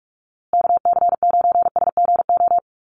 Morse code, heard as a keyed tone, about 44 wpm, 700 Hz. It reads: KC9SGO
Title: USA Call Signs Repeated 3x 45wpm